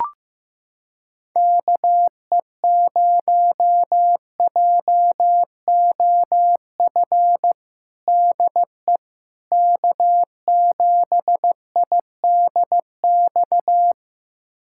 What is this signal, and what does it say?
Morse code, heard as a keyed tone, about 15 wpm, 700 Hz. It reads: KE0JOF DE K7IDX
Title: USA Callsign DE Callsign 15wpm